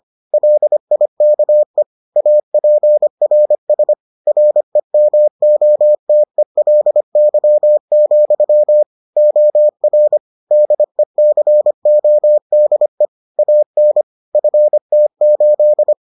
Morse code, heard as a keyed tone, about 25 wpm, 600 Hz. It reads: LIKE APRS REMOTELY, OR DECODE AN FT8